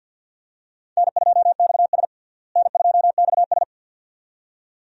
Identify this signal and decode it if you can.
Morse code, heard as a keyed tone, about 51 wpm, 700 Hz. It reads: N2XS N2XS
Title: USA Call Signs - 4 Characters - Repeated 2x - 50wpm